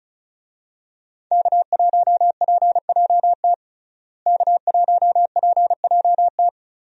Morse code, heard as a keyed tone, about 36 wpm, 700 Hz. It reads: K1PJT K1PJT
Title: USA Call Signs - 5 Characters - Repeated 3x - 35wpm